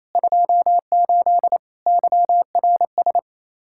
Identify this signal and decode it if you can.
Morse code, heard as a keyed tone, about 28 wpm, 700 Hz. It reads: 28 YRS
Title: Practice QSO - Exchange 3 - More Info - 28wpm